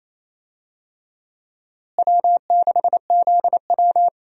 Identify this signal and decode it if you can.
Morse code, heard as a keyed tone, about 28 wpm, 700 Hz. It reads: W6ZW
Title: USA Call Signs - 4 Characters - Repeated 2x - 28wpm